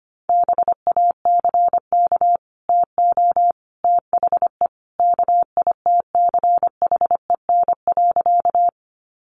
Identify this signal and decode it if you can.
Morse code, heard as a keyed tone, about 25 wpm, 700 Hz. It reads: BACK TO THE KITCHEN.
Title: Sentences from Top 1400 Words 25wpm